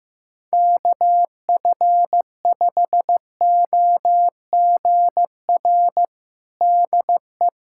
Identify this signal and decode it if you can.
Morse code, heard as a keyed tone, about 15 wpm, 700 Hz. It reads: KF5OGR DE
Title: USA Callsign DE Callsign - 6 Characters - 15wpm